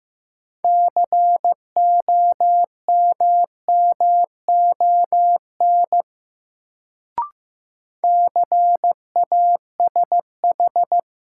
Morse code, heard as a keyed tone, about 15 wpm, 700 Hz. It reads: COMMON CASH
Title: Sentences from Top 700 Words - New Word Drill - 15wpm